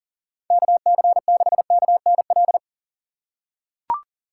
Morse code, heard as a keyed tone, about 40 wpm, 700 Hz. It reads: KC6KNL
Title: USA Call Signs - 6 Characters - 4x Character Spacing - 40wpm